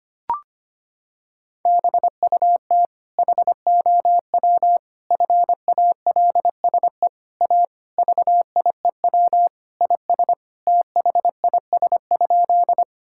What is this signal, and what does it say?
Morse code, heard as a keyed tone, about 25 wpm, 700 Hz. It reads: BUT HOW FALSE A VIEW IS THIS?
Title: Sentences from Top 1600 Words 25wpm